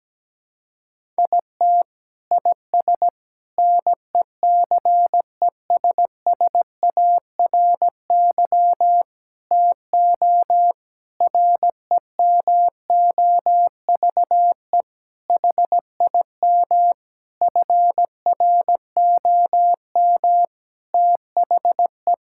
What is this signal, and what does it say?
Morse code, heard as a keyed tone, about 17 wpm, 700 Hz. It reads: IT IS NECESSARY TO REMOVE HIM FROM THE
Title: Sentences from Top 1000 Words - Review Part 5 - 17wpm